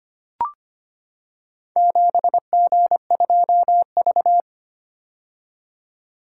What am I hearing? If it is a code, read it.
Morse code, heard as a keyed tone, about 25 wpm, 700 Hz. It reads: 7G2V
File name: practice / 25wpm / Four Letter-Number Combinations - 25wpm